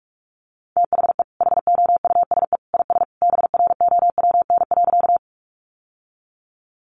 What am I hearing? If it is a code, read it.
Morse code, heard as a keyed tone, about 44 wpm, 700 Hz. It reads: THE HOUSE IS BROWN.
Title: Sentences from Top 1200 Words 45wpm